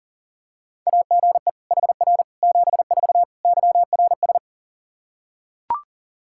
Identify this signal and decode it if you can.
Morse code, heard as a keyed tone, about 40 wpm, 700 Hz. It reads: AGE HR 74 YRS
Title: Practice QSO - Exchange 3 - More Info - 40wpm